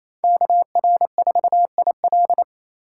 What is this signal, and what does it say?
Morse code, heard as a keyed tone, about 28 wpm, 700 Hz. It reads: KR4IL
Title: USA Call Signs - 5 Characters - 2x Character Spacing - 28wpm